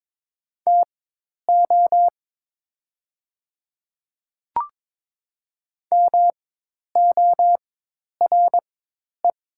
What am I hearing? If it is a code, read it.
Morse code, heard as a keyed tone, about 22 wpm, spaced as for 10 wpm, 700 Hz. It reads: TO MORE
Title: 4x Character Spacing - Top 100 Words - 22wpm